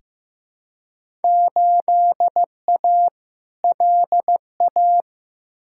Morse code, heard as a keyed tone, about 15 wpm, 700 Hz. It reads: 8A LA
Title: Field Day Exchanges Encore - 15wpm